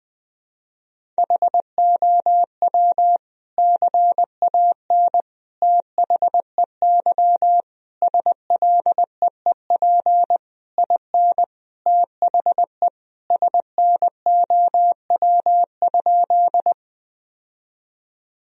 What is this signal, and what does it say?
Morse code, heard as a keyed tone, about 21 wpm, 700 Hz. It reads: HOW CAN THEY SLEEP IN THE SNOW?
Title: Sentences from Top 1000 Words - Review Part 3 - 20wpm